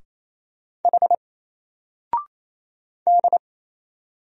Morse code, heard as a keyed tone, about 28 wpm, 700 Hz. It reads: H D
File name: Single-Letter-Number-28wpm